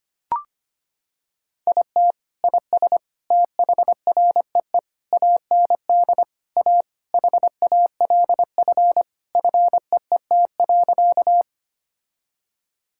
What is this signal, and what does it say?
Morse code, heard as a keyed tone, about 25 wpm, 700 Hz. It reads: IT IS THREE AND A HALF FEET.
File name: practice / 25wpm / Sentences with 3-Word Phrases - 25wpm